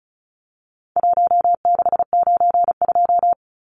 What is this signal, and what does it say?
Morse code, heard as a keyed tone, about 34 wpm, 700 Hz. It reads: 1692
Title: Numbers - 4-Digit 35wpm